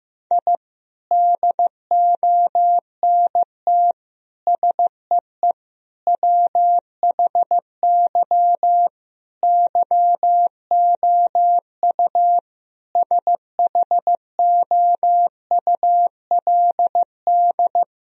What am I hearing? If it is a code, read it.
Morse code, heard as a keyed tone, about 15 wpm, 700 Hz. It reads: I DONT SEE WHY YOU SHOULD
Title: Sentences from Top 1000 Words - Review Part 2 - 15wpm